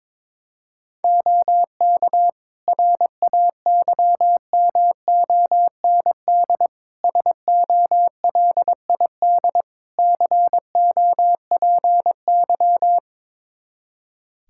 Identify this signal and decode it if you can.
Morse code, heard as a keyed tone, about 22 wpm, 700 Hz. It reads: OK RAYMOND SOLID COPY
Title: Practice QSO - Exchange 3 - More Info - 22wpm